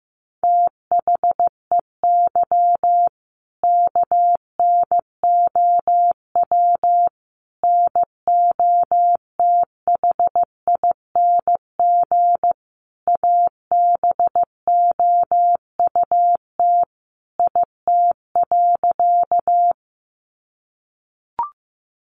Morse code, heard as a keyed tone, about 15 wpm, 700 Hz. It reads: THEY KNOW NOTHING ABOUT IT.
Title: Sentences from Top 500 Words 15wpm